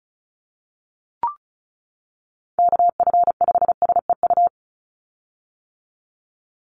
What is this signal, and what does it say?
Morse code, heard as a keyed tone, about 34 wpm, 700 Hz. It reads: KF5SEU
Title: USA Call Signs 35wpm